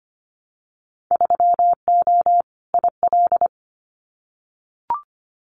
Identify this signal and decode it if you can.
Morse code, heard as a keyed tone, about 25 wpm, 700 Hz. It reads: 3O IL
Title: Winter Field Day Exchanges 25wpm